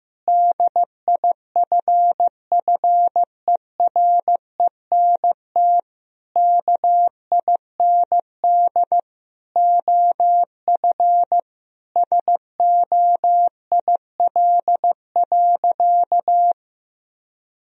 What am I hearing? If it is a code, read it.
Morse code, heard as a keyed tone, about 15 wpm, 700 Hz. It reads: DIFFERENT KIND OF SOIL.
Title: Sentences from Top 1000 Words - Review Part 4 - 15wpm